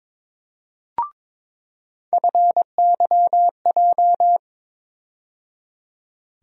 Morse code, heard as a keyed tone, about 22 wpm, 700 Hz. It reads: FYJ
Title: USA Call Sign Postfixes - 3 Letters 22wpm